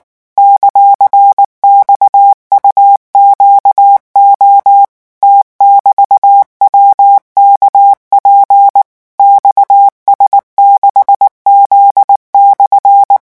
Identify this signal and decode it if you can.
Morse code, heard as a keyed tone, about 19 wpm, 800 Hz. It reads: ;XUQO T<BT>WKP XS6Z/
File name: morse mp3 / revision14-vitesse 18 mots